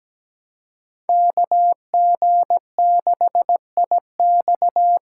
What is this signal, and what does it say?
Morse code, heard as a keyed tone, about 20 wpm, 700 Hz. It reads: KG6IX
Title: Speed Racing - USA 2x2 Call Signs 17wpm